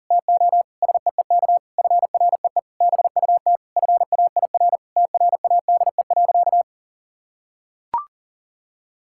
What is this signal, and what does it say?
Morse code, heard as a keyed tone, about 40 wpm, 700 Hz. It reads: TO SEEK FREE BUT FAIR TRADE.
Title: Sentences from Top 1600 Words 40wpm